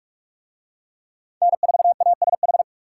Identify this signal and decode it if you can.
Morse code, heard as a keyed tone, about 44 wpm, 700 Hz. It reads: N4ASH